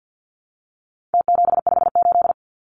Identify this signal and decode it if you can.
Morse code, heard as a keyed tone, about 48 wpm, 700 Hz. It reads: T758
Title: Numbers - Contest Style - 4-Digit 50wpm